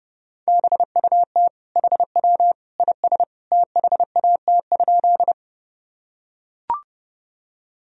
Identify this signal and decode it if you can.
Morse code, heard as a keyed tone, about 30 wpm, 700 Hz. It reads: BUT HW IS THAT?
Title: Sentences from Top 100 Words and Abbreviations 30wpm